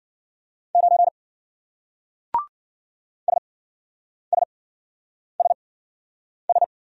Morse code, heard as a keyed tone, about 53 wpm, spaced as for 9 wpm, 700 Hz. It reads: 9 SSSS